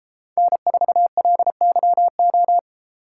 Morse code, heard as a keyed tone, about 33 wpm, 700 Hz. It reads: N4LYO